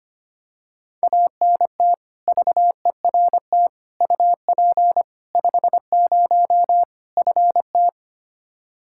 Morse code, heard as a keyed tone, about 25 wpm, 700 Hz. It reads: ANT VERT UP 50 FT
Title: Practice QSO - Exchange 2 - Info - 25wpm